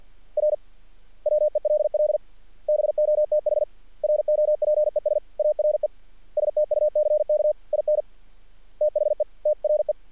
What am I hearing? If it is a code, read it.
Morse code, heard as a keyed tone, about 48 wpm, 600 Hz. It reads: R WELL BOTH ROPES ARE STUCK IN THE TRE
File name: archived / CRSnet